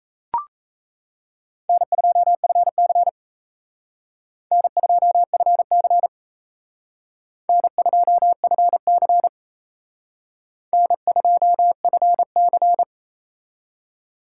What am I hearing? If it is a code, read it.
Morse code, heard as a keyed tone, about 35 wpm, 700 Hz. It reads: N2FC N2FC N2FC N2FC